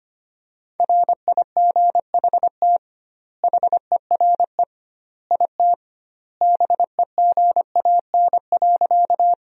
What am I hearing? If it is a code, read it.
Morse code, heard as a keyed tone, about 25 wpm, 700 Hz. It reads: RIGHT HERE IT BEGAN.
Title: 2x Word-Spacing - Sentences from Top 300 Words - 25wpm